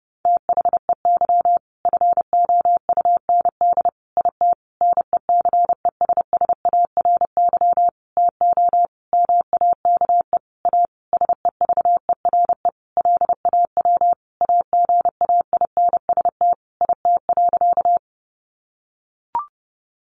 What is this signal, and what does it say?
Morse code, heard as a keyed tone, about 28 wpm, 700 Hz. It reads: THEY FOUND IT NECESSARY TO MAKE A SEVERE LAW AGAINST IT.
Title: Sentences from Top 1400 Words 30wpm